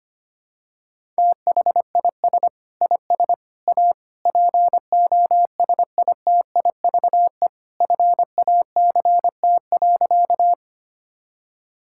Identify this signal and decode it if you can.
Morse code, heard as a keyed tone, about 25 wpm, 700 Hz. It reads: THIS IS A POSITIVE FACT.
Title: Sentences from Top 1000 Words - Review Part 5 - 25wpm